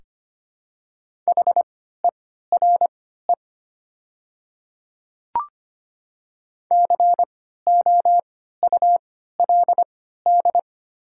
Morse code, heard as a keyed tone, about 25 wpm, spaced as for 14 wpm, 700 Hz. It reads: HERE COULD
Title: 3x Character Spacing - Top 200 Words - 25wpm